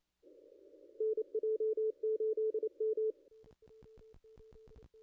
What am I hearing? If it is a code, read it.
Morse code, heard as a keyed tone, about 29 wpm, 440 Hz. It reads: NJ8M
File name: contests / ss-cw-k5tr-op-n5rz -0200